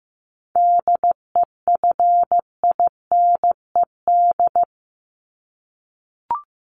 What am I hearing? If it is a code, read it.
Morse code, heard as a keyed tone, about 15 wpm, 700 Hz. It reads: DEFINED
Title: ED Suffix - Top 100 Words with Base Words 15wpm